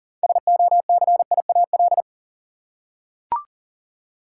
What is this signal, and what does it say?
Morse code, heard as a keyed tone, about 37 wpm, 700 Hz. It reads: SOCIAL
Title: Practice QSO - Exchange 3 - More Info - 40wpm